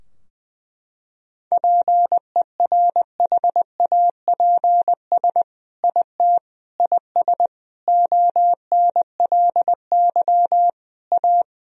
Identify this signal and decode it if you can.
Morse code, heard as a keyed tone, about 21 wpm, 700 Hz. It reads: PERHAPS IT IS ONLY A
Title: Sentences from Top 1000 Words - Review Part 2 - 20wpm